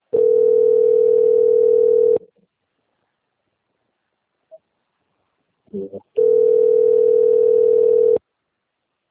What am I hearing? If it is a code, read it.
Ringback tone (North America), 440 + 480 Hz.
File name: out- - - - - .1965